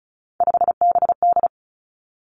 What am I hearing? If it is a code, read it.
Morse code, heard as a keyed tone, about 36 wpm, 700 Hz. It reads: 5BD
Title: Three Challenging Letter-Number Combinations 35wpm